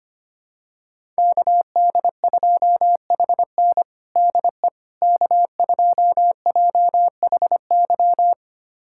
Morse code, heard as a keyed tone, about 25 wpm, 700 Hz. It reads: KD2HN DE K2JHY
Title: USA Callsign DE Callsign - 5 Characters - 25wpm